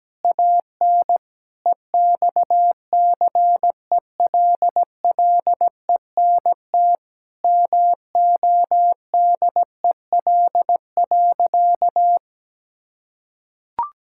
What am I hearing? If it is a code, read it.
Morse code, heard as a keyed tone, about 17 wpm, 700 Hz. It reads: AN EXCELLENT MODEL.
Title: Sentences from Top 1000 Words - Review Part 5 - 17wpm